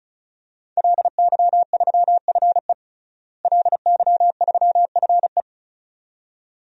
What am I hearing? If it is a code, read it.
Morse code, heard as a keyed tone, about 36 wpm, 700 Hz. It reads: LY3FE LY3FE
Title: DXCC-100 Call Signs - 5 Characters - Repeated 2x - 35wpm